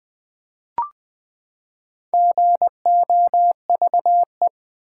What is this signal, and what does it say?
Morse code, heard as a keyed tone, about 20 wpm, 700 Hz. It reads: GOVE
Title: Build A Word - 4 Letters At A Time - 10-Letter Words - 20wpm